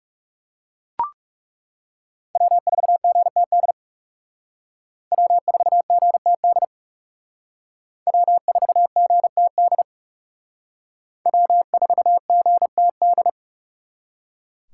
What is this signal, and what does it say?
Morse code, heard as a keyed tone, about 39 wpm, 700 Hz. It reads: W4GTD W4GTD W4GTD W4GTD